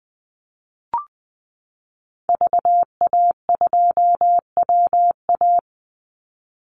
Morse code, heard as a keyed tone, about 20 wpm, 700 Hz. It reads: VA2WA
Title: Call Signs Repeated 3x - 20wpm